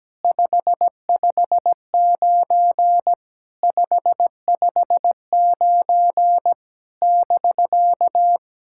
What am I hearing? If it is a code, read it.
Morse code, heard as a keyed tone, about 17 wpm, 700 Hz. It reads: 559 559 <BK>
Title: SOTA QSO Practice 17wpm